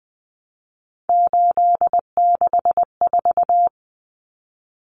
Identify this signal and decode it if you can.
Morse code, heard as a keyed tone, about 20 wpm, 700 Hz. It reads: 864
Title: Common Cut Numbers - 3-Digit 20wpm